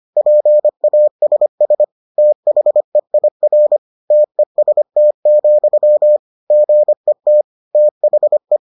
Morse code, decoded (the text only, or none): PASS THEIR TEST, GET THE